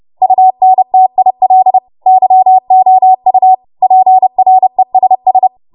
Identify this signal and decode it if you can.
Morse code, heard as a keyed tone, about 30 wpm, 750 Hz. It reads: UNTIL YOU PRESS
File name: Morse / 30WPM